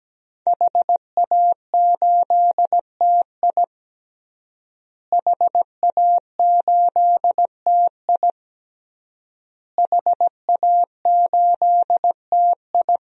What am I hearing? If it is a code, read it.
Morse code, heard as a keyed tone, about 17 wpm, 700 Hz. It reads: HA8TI HA8TI HA8TI
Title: DXCC-100 Call Signs - 5 Characters - Repeated 3x - 17wpm